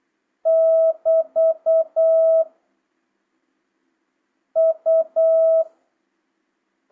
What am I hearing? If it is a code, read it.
Morse code, heard as a keyed tone, about 8 wpm, 650 Hz. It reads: <BT> U